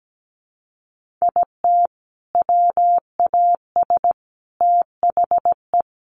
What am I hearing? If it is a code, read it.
Morse code, heard as a keyed tone, about 17 wpm, 700 Hz. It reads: IT WAS THE
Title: Sentences from Top 1500 Words 17wpm